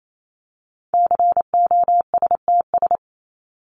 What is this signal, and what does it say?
Morse code, heard as a keyed tone, about 28 wpm, 700 Hz. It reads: COSTS
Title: ES:S Suffix - Top 100 Words with Base Words 28wpm